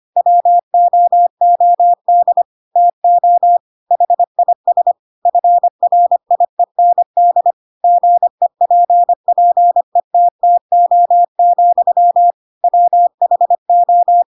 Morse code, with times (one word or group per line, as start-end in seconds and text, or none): 0.2-2.4 WOOD
2.8-3.6 TO
3.9-4.9 HIS
5.3-7.5 FRIEND
7.8-12.3 GEPPETTO,
12.6-14.2 WHO